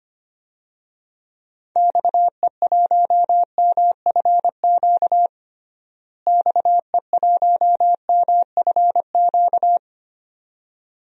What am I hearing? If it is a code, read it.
Morse code, heard as a keyed tone, about 25 wpm, 700 Hz. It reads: XE1MFQ XE1MFQ
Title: DXCC-100 Call Signs - 6 Characters - Repeated 2x - 25wpm